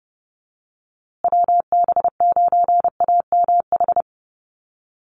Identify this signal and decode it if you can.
Morse code, heard as a keyed tone, about 30 wpm, 700 Hz. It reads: WB9AMH
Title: USA Call Signs Repeated 3x 30wpm